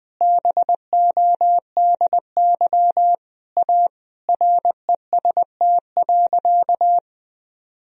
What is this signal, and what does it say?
Morse code, heard as a keyed tone, about 20 wpm, 700 Hz. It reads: BODY A REST.
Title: Sentences from Top 1000 Words - Review Part 2 - 20wpm